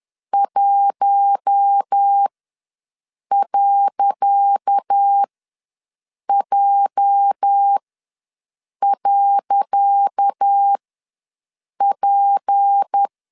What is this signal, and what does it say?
Morse code, heard as a keyed tone, about 10 wpm, spaced as for 6 wpm, 800 Hz. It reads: 1.J.P
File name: LEZIONI / 10 SIGNS / 10C DOT ws 10wpm 4213